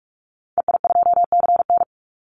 Morse code, heard as a keyed tone, about 44 wpm, 700 Hz. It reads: EI2CN
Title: Call Signs - 45wpm